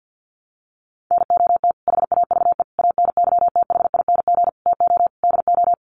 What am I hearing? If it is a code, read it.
Morse code, heard as a keyed tone, about 50 wpm, 700 Hz. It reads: NOT HAVE ANYTHING TO DO
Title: Sets of 5 Words Encore 50wpm